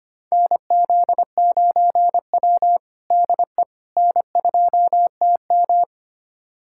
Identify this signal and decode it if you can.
Morse code, heard as a keyed tone, about 25 wpm, 700 Hz. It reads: NZ9W DE N2TM
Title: USA Callsign DE Callsign - 4 Characters - 25wpm